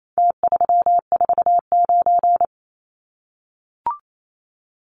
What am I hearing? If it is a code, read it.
Morse code, heard as a keyed tone, about 28 wpm, 700 Hz. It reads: T349
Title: Numbers - Contest Style - 4-Digit 28wpm